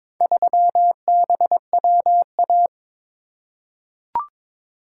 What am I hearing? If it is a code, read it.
Morse code, heard as a keyed tone, about 22 wpm, 700 Hz. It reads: 3BWA
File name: practice / 22wpm / Four Letter-Number Combinations - 22wpm